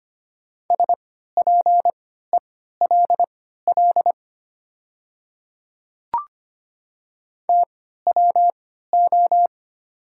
Morse code, heard as a keyed tone, about 25 wpm, spaced as for 14 wpm, 700 Hz. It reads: SPELL TWO
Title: 3x Character Spacing - Top 200 Words - 25wpm